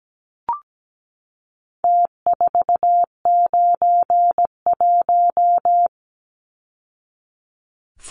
Morse code, heard as a keyed tone, about 17 wpm, 700 Hz. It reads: T491
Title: Numbers - Contest Style - 4-Digit 17wpm